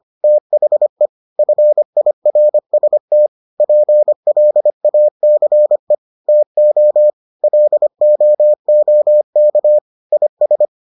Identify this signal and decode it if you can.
Morse code, heard as a keyed tone, about 25 wpm, 600 Hz. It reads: THE FIRST PLACE TO LOOK IS